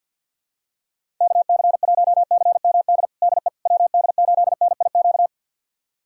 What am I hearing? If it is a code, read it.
Morse code, heard as a keyed tone, about 51 wpm, 700 Hz. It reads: KC1CMD DE WD8NIQ
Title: USA Callsign DE Callsign 50wpm